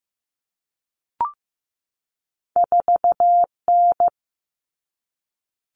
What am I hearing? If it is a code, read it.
Morse code, heard as a keyed tone, about 15 wpm, 700 Hz. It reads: 4N
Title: All Cut Numbers - 2-Digit 15wpm